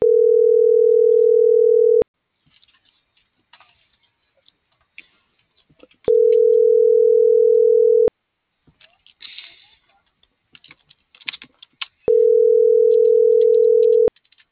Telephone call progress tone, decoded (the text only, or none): ringback tone